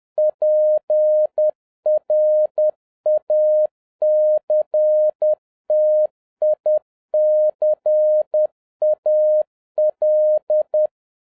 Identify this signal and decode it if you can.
Morse code, heard as a keyed tone, about 10 wpm, 610 Hz. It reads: PRACTICAL